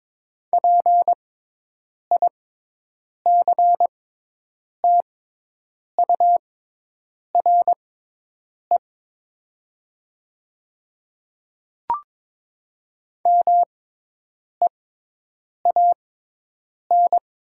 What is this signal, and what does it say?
Morse code, heard as a keyed tone, about 22 wpm, spaced as for 8 wpm, 700 Hz. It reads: PICTURE MEAN
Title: 6x Character Spacing - Top 200 Words - 22wpm